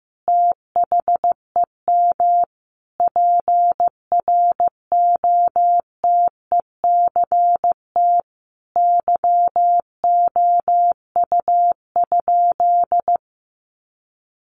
Morse code, heard as a keyed tone, about 15 wpm, 700 Hz. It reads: THEM PROTECT YOU?
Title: Sentences from Top 900 Words 15wpm